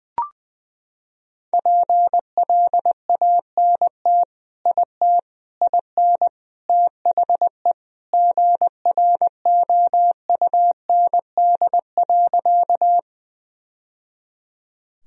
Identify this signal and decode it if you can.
Morse code, heard as a keyed tone, about 20 wpm, 700 Hz. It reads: PLANT IT IN THE GROUND.
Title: Sentences from Top 1000 Words - Review Part 1 - 20wpm